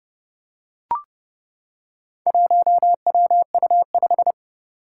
Morse code, heard as a keyed tone, about 30 wpm, 700 Hz. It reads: 1WU5
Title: Four Challenging Letter-Number Combinations 30wpm